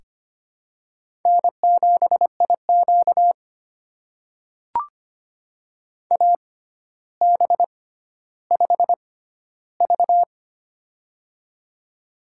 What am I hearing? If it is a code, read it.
Morse code, heard as a keyed tone, about 25 wpm, spaced as for 13 wpm, 700 Hz. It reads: N7IQ AB5V
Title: USA Call Signs - 4 Characters - 6x Character Spacing - 25wpm